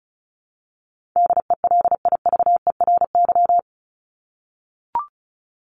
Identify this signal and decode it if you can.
Morse code, heard as a keyed tone, about 34 wpm, 700 Hz. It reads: DELIVERY
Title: Top 1251-1500 Words 35wpm